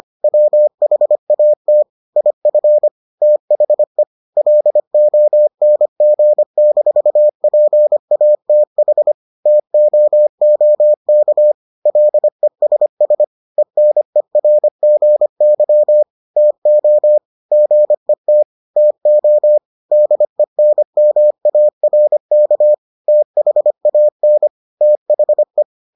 Morse code, heard as a keyed tone, about 25 wpm, 600 Hz. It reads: WHAT IF THE LONG-PATH TOOK LESS ENERGY TO GET TO DENMARK THAN THE